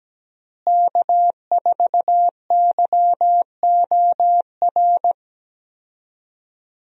Morse code, heard as a keyed tone, about 17 wpm, 700 Hz. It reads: K4YOR
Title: USA Call Signs - 5 Characters - Repeated 2x - 17wpm